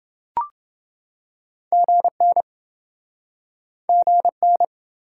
Morse code, heard as a keyed tone, about 28 wpm, 700 Hz. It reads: GN GN